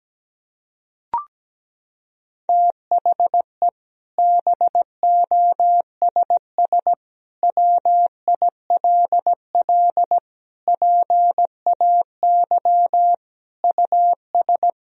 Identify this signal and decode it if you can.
Morse code, heard as a keyed tone, about 17 wpm, 700 Hz. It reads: THE BOSS WILL PAY US
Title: Sentences from Top 1000 Words - Review Part 4 - 17wpm